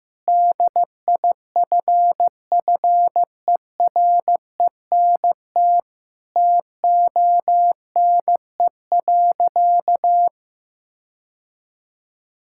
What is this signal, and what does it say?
Morse code, heard as a keyed tone, about 16 wpm, 700 Hz. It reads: DIFFERENT TONE.
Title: Sentences from Top 1600 Words 15wpm